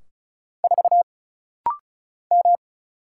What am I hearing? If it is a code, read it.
Morse code, heard as a keyed tone, about 34 wpm, 700 Hz. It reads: 4 M
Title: Single Letter-Number - Rapid-Fire 35wpm